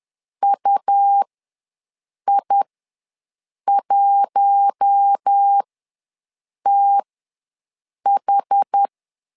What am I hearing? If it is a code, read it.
Morse code, heard as a keyed tone, about 10 wpm, spaced as for 6 wpm, 800 Hz. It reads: UI1TH
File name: LEZIONI / 4 Q1 ws 10wpm 2538